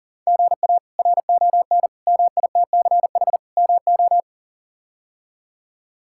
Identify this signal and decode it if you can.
Morse code, heard as a keyed tone, about 40 wpm, 700 Hz. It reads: GA RON MITCH MO
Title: K1USN SST QSO Practice - 40wpm